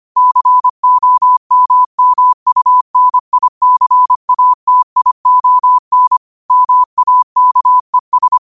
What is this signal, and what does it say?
Morse code, heard as a keyed tone, about 25 wpm, 1000 Hz. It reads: COMMUNICATION MAKES
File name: qotdfiles / 25wpm